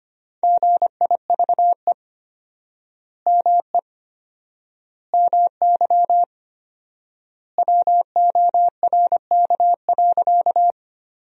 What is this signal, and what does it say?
Morse code, heard as a keyed tone, about 25 wpm, 700 Hz. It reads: GIVE ME MY WORK.
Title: 4x Word-Spacing - Sentences from Top 200 Words - 25wpm